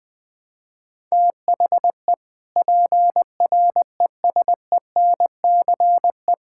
Morse code, heard as a keyed tone, about 20 wpm, 700 Hz. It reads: THE PRESENCE
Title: Sentences with 3-Word Phrases - Spoken Beginning - 20wpm